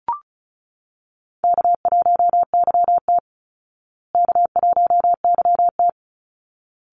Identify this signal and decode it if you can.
Morse code, heard as a keyed tone, about 34 wpm, 700 Hz. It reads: K1YT K1YT
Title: Call Signs Repeated 3x - 35wpm